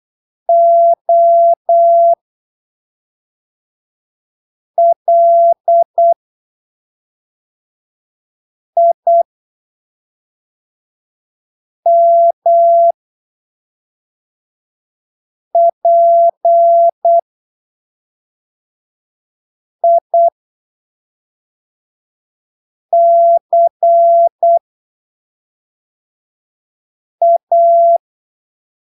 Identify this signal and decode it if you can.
Morse code, heard as a keyed tone, about 8 wpm, spaced as for 3 wpm, 680 Hz. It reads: OLIMPICA
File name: arquivos / texto / texto 08 08ppm